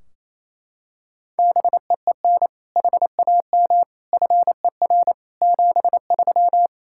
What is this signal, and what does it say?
Morse code, heard as a keyed tone, about 28 wpm, 700 Hz. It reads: BEEN HAM FER 73
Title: K1USN SST QSO Practice 28wpm